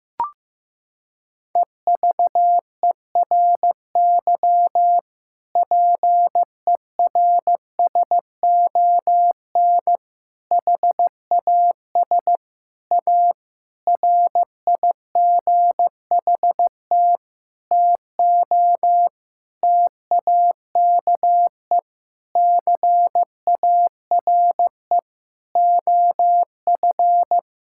Morse code, heard as a keyed tone, about 16 wpm, 700 Hz. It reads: EVERY PERSON HAS A RIGHT TO TAKE CARE OF
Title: Sentences from Top 1000 Words - Review Part 5 - 15wpm